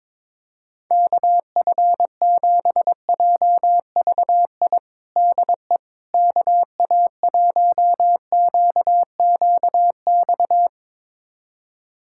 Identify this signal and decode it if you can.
Morse code, heard as a keyed tone, about 22 wpm, 700 Hz. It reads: KF7JVI DE KA1QQX
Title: USA Callsign DE Callsign - 6 Characters - 22wpm